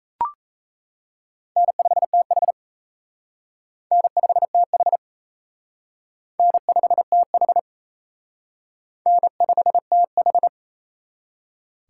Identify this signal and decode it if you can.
Morse code, heard as a keyed tone, about 34 wpm, 700 Hz. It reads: N5TH N5TH N5TH N5TH
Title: Speed Racing - USA Call Signs 28wpm